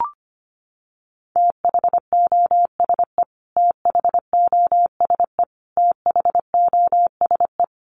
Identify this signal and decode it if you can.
Morse code, heard as a keyed tone, about 25 wpm, 700 Hz. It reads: THOSE THOSE THOSE
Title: Top 500 Words Repeated 25wpm